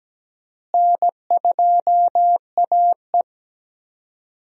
Morse code, heard as a keyed tone, about 17 wpm, 700 Hz. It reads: N2AE
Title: USA Call Signs - 4 Characters - Repeated 3x - 17wpm